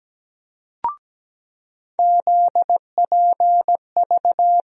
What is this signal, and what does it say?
Morse code, heard as a keyed tone, about 17 wpm, 700 Hz. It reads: ZPV
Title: USA Call Sign Postfixes - 3 Letters 17wpm